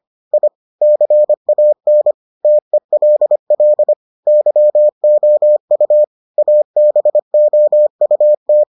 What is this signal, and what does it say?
Morse code, heard as a keyed tone, about 25 wpm, 600 Hz. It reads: I CAN TELL YOU ABOUT